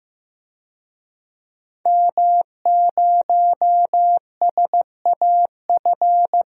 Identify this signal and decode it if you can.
Morse code, heard as a keyed tone, about 15 wpm, 700 Hz. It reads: M0SAF
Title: DXCC-100 Call Signs - 5 Characters - Repeated 2x - 15wpm